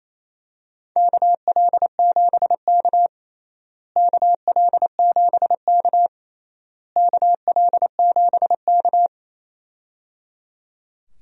Morse code, heard as a keyed tone, about 27 wpm, 700 Hz. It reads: KL7K KL7K KL7K
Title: DXCC-100 Call Signs - 4 Characters - Repeated 3x - 28wpm